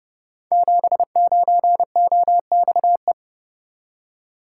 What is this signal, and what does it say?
Morse code, heard as a keyed tone, about 30 wpm, 700 Hz. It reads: 79OXE
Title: Code Groups - 5 Letter:Numbers - 30wpm